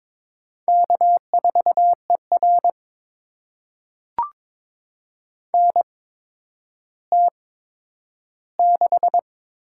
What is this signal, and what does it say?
Morse code, heard as a keyed tone, about 22 wpm, spaced as for 11 wpm, 700 Hz. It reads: K4ER NT6
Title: USA Call Signs - 4 Characters - 8x Character Spacing - 22wpm